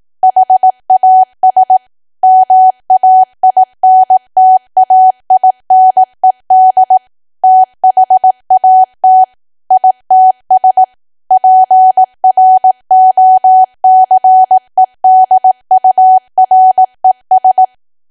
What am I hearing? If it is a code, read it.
Morse code, heard as a keyed tone, about 18 wpm, 750 Hz. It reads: HAS MAINTAINED THAT ITS PROCEDURES
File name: cwa / W1AW-Level2 / 18WPM